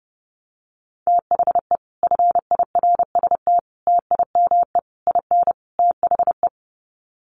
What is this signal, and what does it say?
Morse code, heard as a keyed tone, about 30 wpm, 700 Hz. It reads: THE FIRST TIME IN THE
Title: Sets of 5 Words Encore 30wpm